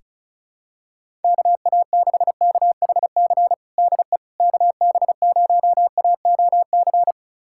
Morse code, heard as a keyed tone, about 36 wpm, 700 Hz. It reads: KA6KHC DE KB0AOC
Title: USA Callsign DE Callsign - 6 Characters - 35wpm